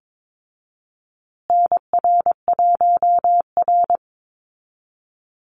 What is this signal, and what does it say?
Morse code, heard as a keyed tone, about 22 wpm, 700 Hz. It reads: NR1R
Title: Call Signs Repeated 2x - 22wpm